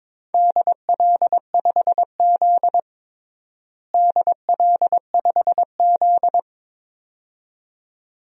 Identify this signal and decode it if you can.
Morse code, heard as a keyed tone, about 22 wpm, 700 Hz. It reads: DL5Z DL5Z
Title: DXCC-100 Call Signs - 4 Characters - Repeated 2x - 22wpm